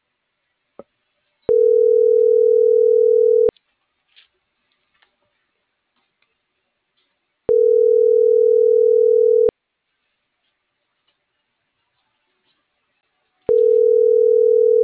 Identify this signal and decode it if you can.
Ringback tone (North America), 440 + 480 Hz.